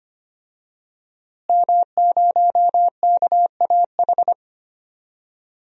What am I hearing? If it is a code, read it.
Morse code, heard as a keyed tone, about 25 wpm, 700 Hz. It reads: M0KAH
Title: DXCC-100 Call Signs - 5 Characters - Repeated 3x - 25wpm